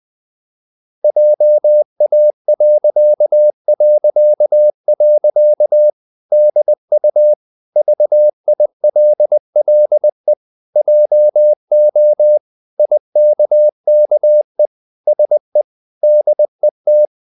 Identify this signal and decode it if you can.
Morse code, heard as a keyed tone, about 20 wpm, 600 Hz. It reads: JA... DU VILLE JO IKKE SE DET
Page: Ferie 20 wpm